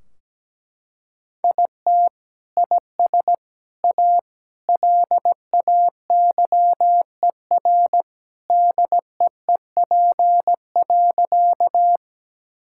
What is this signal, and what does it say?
Morse code, heard as a keyed tone, about 17 wpm, 700 Hz. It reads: IT IS A LAYER DEEP.
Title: Sentences from Top 1600 Words 17wpm